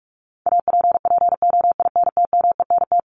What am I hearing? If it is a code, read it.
Morse code, heard as a keyed tone, about 44 wpm, 700 Hz. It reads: APPOINTMENT
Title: Top 1751-2000 Words 45wpm